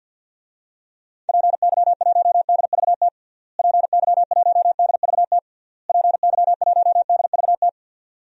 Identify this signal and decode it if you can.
Morse code, heard as a keyed tone, about 52 wpm, 700 Hz. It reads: PY1DVT PY1DVT PY1DVT
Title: DXCC-100 Call Signs - 6 Characters - Repeated 3x - 50wpm